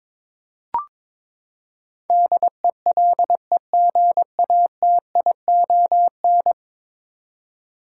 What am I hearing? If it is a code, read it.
Morse code, heard as a keyed tone, about 21 wpm, 700 Hz. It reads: DELEGATION
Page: morse-code-ninja-practice-22wpm